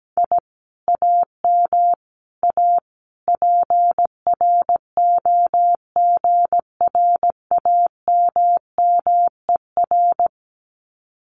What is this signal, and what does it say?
Morse code, heard as a keyed tone, about 17 wpm, 700 Hz. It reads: I AM A PROGRAMMER
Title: Common QSO Phrases Encore 17wpm